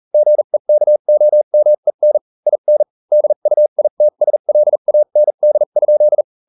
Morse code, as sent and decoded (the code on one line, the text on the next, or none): --. -.- --- -- . -. / .. -. / -.. ..- .. - ... .-.. .- -. -.. ..--..
GEKOMEN IN DUITSLAND?